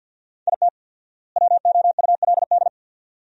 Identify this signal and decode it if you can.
Morse code, heard as a keyed tone, about 50 wpm, 700 Hz. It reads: IT WOULD
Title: 4x Word-Spacing - Sentences from Top 200 Words - 50wpm